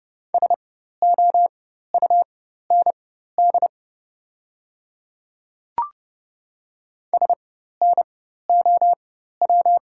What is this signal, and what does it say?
Morse code, heard as a keyed tone, about 30 wpm, spaced as for 14 wpm, 690 Hz. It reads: SOUND SNOW
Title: 4x Character Spacing - Top 500 Words - 30wpm